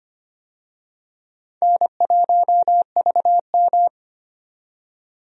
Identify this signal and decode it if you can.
Morse code, heard as a keyed tone, about 25 wpm, 700 Hz. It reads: N1VM
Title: USA Call Signs - 4 Characters - 25wpm